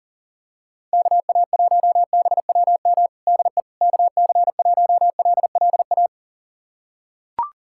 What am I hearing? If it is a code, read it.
Morse code, heard as a keyed tone, about 40 wpm, 700 Hz. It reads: KA1BWM DE KC1LLA
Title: USA Callsign DE Callsign 40wpm